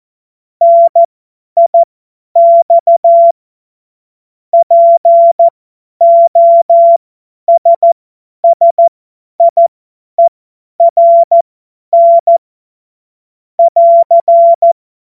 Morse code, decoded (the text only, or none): NIX POSSIERN <AR>